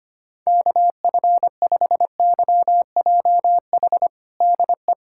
Morse code, heard as a keyed tone, about 25 wpm, 700 Hz. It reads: KF5YJH DE
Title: USA Callsign DE Callsign - 6 Characters - 25wpm